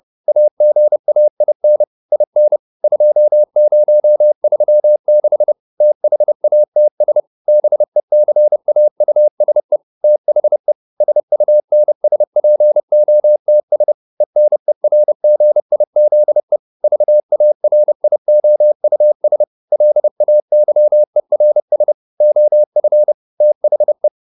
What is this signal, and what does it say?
Morse code, heard as a keyed tone, about 30 wpm, 600 Hz. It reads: AGAIN IN 2036 THATS BECAUSE THE SUNSPOTS ENERGIZE VARIOUS LAYERS OF THE